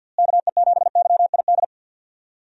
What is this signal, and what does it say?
Morse code, heard as a keyed tone, about 47 wpm, 700 Hz. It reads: KE7YID
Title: USA Call Signs - 6 Characters - Repeated 3x - 50wpm